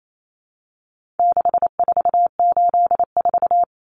Morse code, heard as a keyed tone, about 28 wpm, 700 Hz. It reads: 6484
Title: Numbers - 4-Digit 28wpm